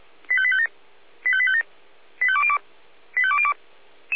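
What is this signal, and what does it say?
Five-tone selective call (ZVEI), heard as quick runs of five tones, tones of about 70 ms. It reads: 85857 85857 85202 85202
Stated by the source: ZVEI five-tone selective call 85810 received by SDR, logged by Feuerwehr Orlamünde fire brigade.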